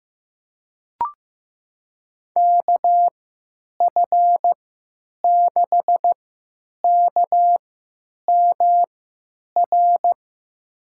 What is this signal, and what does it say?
Morse code, heard as a keyed tone, about 15 wpm, spaced as for 9 wpm, 700 Hz. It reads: KF6KMR
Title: USA Call Signs - 6 Characters - 3x Character Spacing - 15wpm